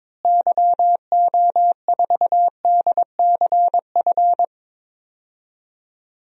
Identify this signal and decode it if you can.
Morse code, heard as a keyed tone, about 22 wpm, 700 Hz. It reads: YO4DCF
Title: DXCC-100 Call Signs - 6 Characters - Repeated 2x - 22wpm